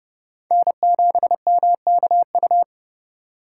Morse code, heard as a keyed tone, about 30 wpm, 700 Hz. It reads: N7MKU